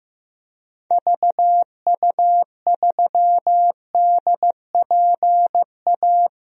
Morse code, heard as a keyed tone, about 15 wpm, 700 Hz. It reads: VU3DPA
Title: DXCC-100 Call Signs - 6 Characters - 15wpm